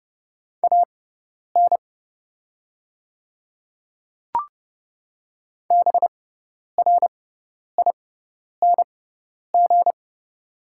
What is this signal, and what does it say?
Morse code, heard as a keyed tone, about 30 wpm, spaced as for 10 wpm, 690 Hz. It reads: AN BRING
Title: 6x Character Spacing - Top 500 Words - 30wpm